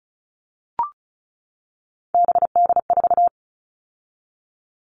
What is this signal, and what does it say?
Morse code, heard as a keyed tone, about 34 wpm, 700 Hz. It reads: BD4
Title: Callsign Prefixes for Top 100 DXCC - 35wpm